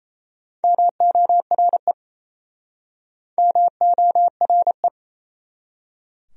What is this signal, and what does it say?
Morse code, heard as a keyed tone, about 38 wpm, 700 Hz. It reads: MORE MORE